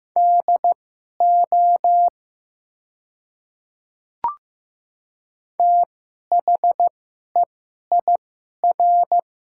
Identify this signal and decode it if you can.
Morse code, heard as a keyed tone, about 15 wpm, spaced as for 11 wpm, 700 Hz. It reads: DO THEIR
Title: 2x Character Spacing - Top 200 Words - 15wpm